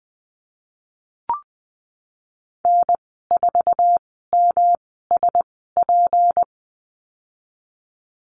Morse code, heard as a keyed tone, about 20 wpm, spaced as for 15 wpm, 700 Hz. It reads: N4MSP
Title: 179 Lesson 29g - Slash - Call Signs 20wpm